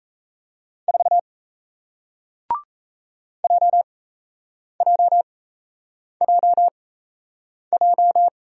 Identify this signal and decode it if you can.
Morse code, heard as a keyed tone, about 38 wpm, spaced as for 8 wpm, 700 Hz. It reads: 4 JJJJ